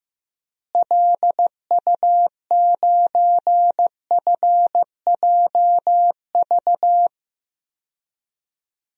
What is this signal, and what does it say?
Morse code, heard as a keyed tone, about 15 wpm, 700 Hz. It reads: LU9FJV
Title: DXCC-100 Call Signs - 6 Characters - Repeated 3x - 15wpm